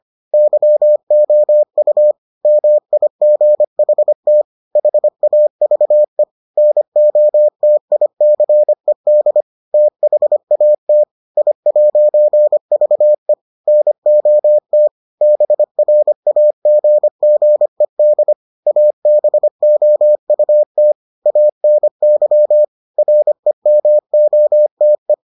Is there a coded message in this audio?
Yes: Morse code from 0.3 to 25.1 s, heard as a keyed tone, about 25 wpm, 600 Hz. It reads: YOU MIGHT HAVE NOTICED THAT I'VE NOT BRAGGED ABOUT ANY REMOTE